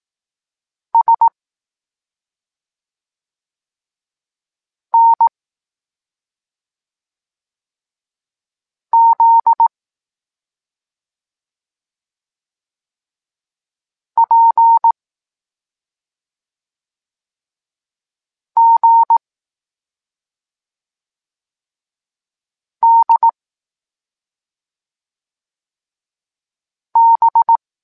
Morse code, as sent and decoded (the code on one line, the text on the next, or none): ... / -. / --.. / .--. / --. / -.. / -...
S N Z P G D B